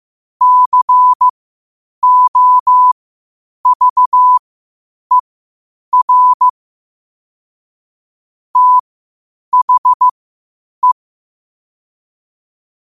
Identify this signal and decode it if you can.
Morse code, heard as a keyed tone, about 15 wpm, spaced as for 8 wpm, 1000 Hz. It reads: COVER THE
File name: qotdfiles / 8wpm